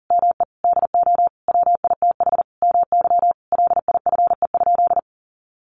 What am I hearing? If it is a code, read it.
Morse code, heard as a keyed tone, about 40 wpm, 700 Hz. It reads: ME DO WITH MY LIFE?
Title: Sentences from Top 300 Words 40wpm